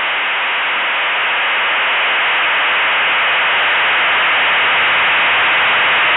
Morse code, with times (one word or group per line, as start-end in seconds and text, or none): none